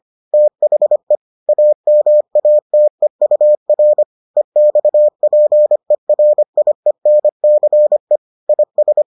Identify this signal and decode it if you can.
Morse code, heard as a keyed tone, about 25 wpm, 600 Hz. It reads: THE AMATEUR EXPERIENCE IS